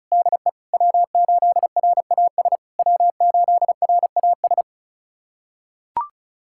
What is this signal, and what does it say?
Morse code, heard as a keyed tone, about 36 wpm, 690 Hz. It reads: DE W8RAS W8RAS
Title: SOTA QSO Practice 35wpm